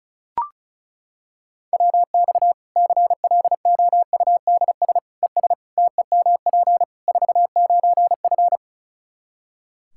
Morse code, heard as a keyed tone, about 34 wpm, 700 Hz. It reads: WX CLOUDS ES TEMP 49F
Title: K1USN SST QSO Practice 35wpm